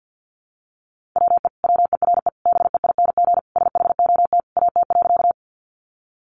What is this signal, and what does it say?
Morse code, heard as a keyed tone, about 48 wpm, 700 Hz. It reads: WE WERE BEING SHOT AT.
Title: Sentences from Top 1100 Words 50wpm